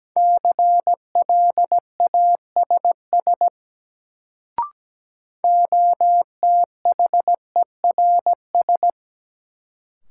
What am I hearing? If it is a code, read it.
Morse code, heard as a keyed tone, about 17 wpm, 700 Hz. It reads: CLASS OTHERS
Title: Sentences from Top 1000 Words - Review Part 2 - New Word Drill - 17wpm